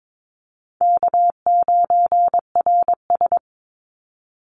Morse code, heard as a keyed tone, about 22 wpm, 700 Hz. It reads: K9RS
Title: Call Signs - 22wpm